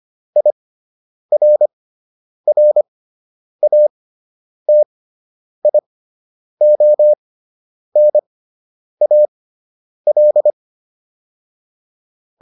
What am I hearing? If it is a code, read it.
Morse code, heard as a keyed tone, about 25 wpm, spaced as for 9 wpm, 600 Hz. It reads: IRRATIONAL